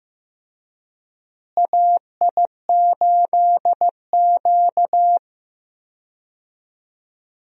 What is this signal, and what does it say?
Morse code, heard as a keyed tone, about 15 wpm, 700 Hz. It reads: AI8Q
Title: USA Call Signs - 4 Characters - Repeated 3x - 15wpm